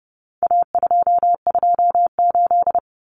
Morse code, heard as a keyed tone, about 30 wpm, 700 Hz. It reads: A228
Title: Common Cut Numbers - 4-Digit 30wpm